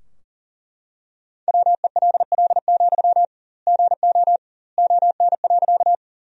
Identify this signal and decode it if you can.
Morse code, heard as a keyed tone, about 40 wpm, 700 Hz. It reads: WELL, GO ON.
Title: 2x Word-Spacing - Sentences from Top 200 Words - 40wpm